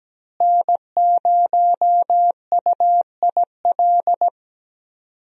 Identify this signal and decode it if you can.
Morse code, heard as a keyed tone, about 17 wpm, 700 Hz. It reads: N0UIL
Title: USA Call Signs - 5 Characters - Repeated 2x - 17wpm